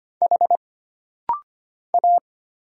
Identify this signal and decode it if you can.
Morse code, heard as a keyed tone, about 25 wpm, 700 Hz. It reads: H A
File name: Single Character (Letter:Number:Punctuation:Prosign) - Rapid-Fire - 25wpm